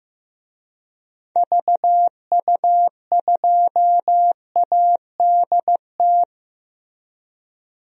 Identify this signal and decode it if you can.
Morse code, heard as a keyed tone, about 16 wpm, 700 Hz. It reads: VU2ADT
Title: DXCC-100 Call Signs - 6 Characters - Repeated 2x - 15wpm